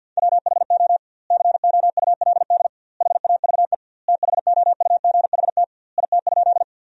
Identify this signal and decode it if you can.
Morse code, heard as a keyed tone, about 47 wpm, 700 Hz. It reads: WHO COULD HAVE THOUGHT IT?
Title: 2x Word-Spacing - Sentences from Top 300 Words - 50wpm